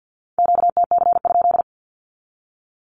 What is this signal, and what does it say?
Morse code, heard as a keyed tone, about 50 wpm, 700 Hz. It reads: QTC?
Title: Q-Codes with Spoken Meaning - 50wpm